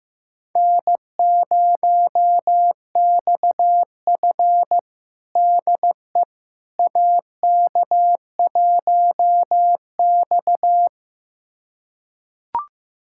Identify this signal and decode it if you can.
Morse code, heard as a keyed tone, about 16 wpm, 700 Hz. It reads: N0XF DE AK1X
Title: USA Callsign DE Callsign - 4 Characters - 15wpm